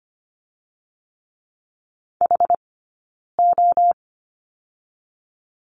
Morse code, heard as a keyed tone, about 25 wpm, 700 Hz. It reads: H O
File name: Lesson 17d - Letter u - Two Characters 25wpm